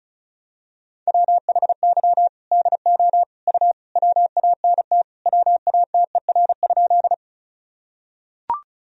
Morse code, heard as a keyed tone, about 36 wpm, 700 Hz. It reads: WHY DO U WANT WATER?
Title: Sentences from Top 200 Words and Abbreviations 35wpm